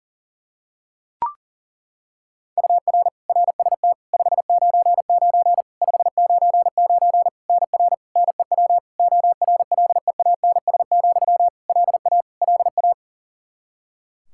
Morse code, heard as a keyed tone, about 37 wpm, 700 Hz. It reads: UR RST 599 599 NR NEW ORLEANS, LA LA
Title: Practice QSO - Exchange 1 - Beginning - 40wpm